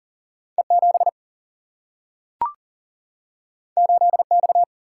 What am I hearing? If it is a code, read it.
Morse code, heard as a keyed tone, about 40 wpm, 700 Hz. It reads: E 7 8 X
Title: Single-Letter Single-Number 40wpm